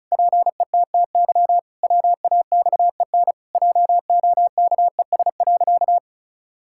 Morse code, heard as a keyed tone, about 34 wpm, 700 Hz. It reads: PETTY WAXEN JOKES.
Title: Pangram Challenge v2 - 35wpm